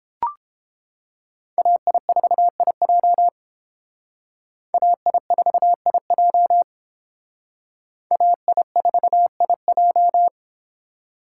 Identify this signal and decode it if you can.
Morse code, heard as a keyed tone, about 31 wpm, 700 Hz. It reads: AI4IJ AI4IJ AI4IJ